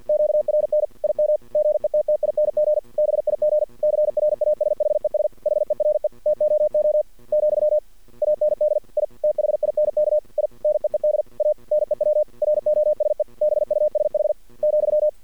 Morse code, heard as a keyed tone, about 53 wpm, 610 Hz. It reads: QRN IM GETTING BIG CRASHES HERE TOO , AND I THINK I NEED A NEW ROSE BUSH ,
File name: archived / CRSnet